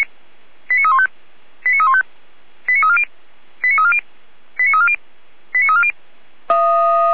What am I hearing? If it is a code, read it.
Five-tone selective call (ZVEI), heard as quick runs of five tones, tones of about 70 ms. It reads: 89315 89315 89340 89340 89340 89340
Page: 455/SDR/ZVEI/89001